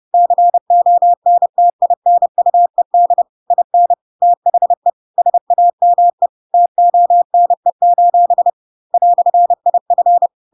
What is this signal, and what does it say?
Morse code, heard as a keyed tone, about 30 wpm, 700 Hz. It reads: CONTINUED IN THE SAME TONE: "IF
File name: Pinocchio / Pinocchio ch4 30 30 ik1hge